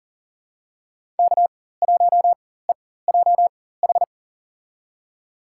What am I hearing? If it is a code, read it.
Morse code, heard as a keyed tone, about 37 wpm, spaced as for 18 wpm, 700 Hz. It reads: K1EJH
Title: USA Call Signs - 5 Characters - 4x Character Spacing - 40wpm